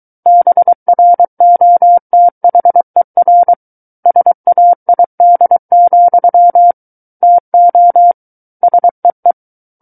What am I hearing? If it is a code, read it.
Morse code, heard as a keyed tone, about 23 wpm, 700 Hz. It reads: BROTHER SAID, TO SEE
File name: Book One / 14-Chapter 14 - 23 WPM